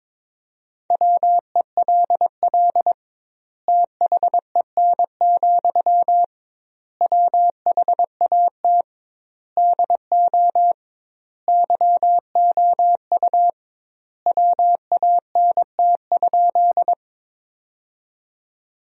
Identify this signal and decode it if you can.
Morse code, heard as a keyed tone, about 22 wpm, 700 Hz. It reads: WELL THEN, WHAT DO YOU WANT?
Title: 2x Word-Spacing - Sentences from Top 200 Words - 22wpm